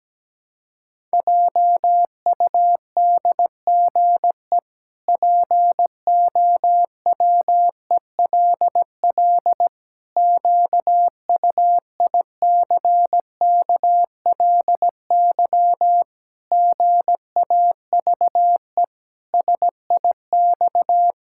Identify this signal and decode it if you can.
Morse code, heard as a keyed tone, about 17 wpm, 700 Hz. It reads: JUDGE POWELL QUICKLY GAVE SIX
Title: Pangram Challenge v2 - 17wpm